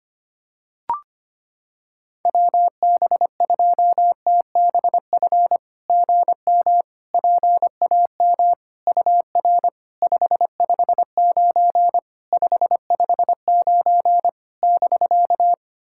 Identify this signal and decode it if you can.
Morse code, heard as a keyed tone, about 25 wpm, 690 Hz. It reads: WB2TBF GM PAM UR 559 559 <BK>
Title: SOTA QSO Practice 25wpm